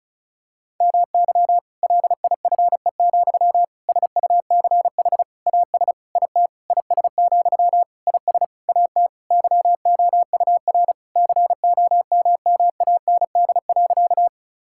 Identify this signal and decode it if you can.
Morse code, heard as a keyed tone, about 34 wpm, 700 Hz. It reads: MY LIFE, SUCH AS IT IS, IS AT YOUR COMMAND.
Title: Sentences from Top 1600 Words 35wpm